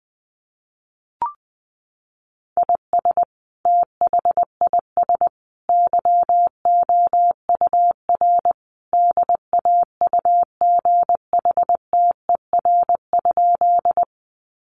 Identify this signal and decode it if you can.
Morse code, heard as a keyed tone, about 20 wpm, 700 Hz. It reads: IS THIS YOUR DAUGHTER?
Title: Sentences from Top 1500 Words 20wpm